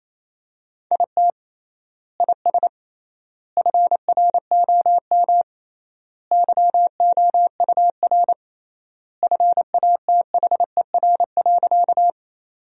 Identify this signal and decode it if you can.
Morse code, heard as a keyed tone, about 28 wpm, 700 Hz. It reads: IT IS FROM YOUR FATHER.
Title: 3x Word-Spacing - Sentences from Top 300 Words - 28wpm